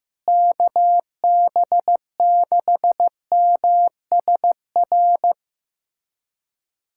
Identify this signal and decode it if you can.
Morse code, heard as a keyed tone, about 15 wpm, 700 Hz. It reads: KB6MSR
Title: USA Callsign DE Callsign - 6 Characters - 15wpm